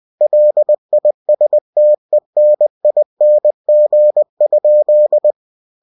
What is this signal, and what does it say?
Morse code, heard as a keyed tone, about 20 wpm, 600 Hz. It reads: LISTENING?
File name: cw-books / winnie-the-pooh / winnie-20wpm-page-0081